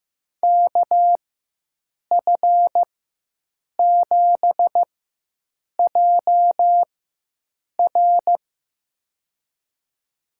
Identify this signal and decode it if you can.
Morse code, heard as a keyed tone, about 15 wpm, spaced as for 7 wpm, 700 Hz. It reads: KF7JR
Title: USA Call Signs - 4x Character Spacing - 15wpm